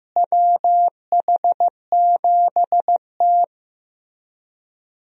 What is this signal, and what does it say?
Morse code, heard as a keyed tone, about 15 wpm, 700 Hz. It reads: WH7T
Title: USA Call Signs - 4 Characters - Repeated 2x - 15wpm